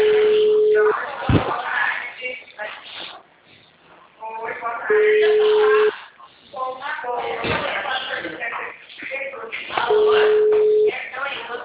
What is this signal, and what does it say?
Ringback tone (Europe, 425 Hz), with speech.